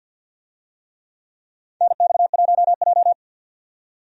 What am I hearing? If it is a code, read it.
Morse code, heard as a keyed tone, about 53 wpm, 700 Hz. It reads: NX1J
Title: USA Call Signs - 4 Characters - 50wpm